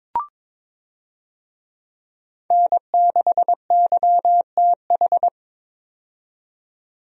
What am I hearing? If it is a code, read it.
Morse code, heard as a keyed tone, about 22 wpm, 700 Hz. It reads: N6YTH
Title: USA Call Signs - 5 Characters - 22wpm